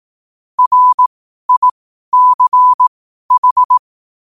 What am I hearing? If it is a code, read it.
Morse code, heard as a keyed tone, about 18 wpm, spaced as for 12 wpm, 1000 Hz. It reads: RICH
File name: qotdfiles / 12wpm